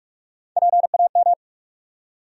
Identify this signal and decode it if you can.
Morse code, heard as a keyed tone, about 43 wpm, 700 Hz. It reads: PAM
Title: USA Common Names - Max 5 Letters - 45wpm